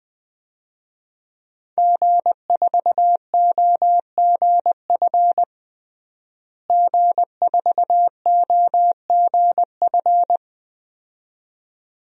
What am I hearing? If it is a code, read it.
Morse code, heard as a keyed tone, about 20 wpm, 700 Hz. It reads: G4OGF G4OGF
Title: DXCC-100 Call Signs - 5 Characters - Repeated 2x - 20wpm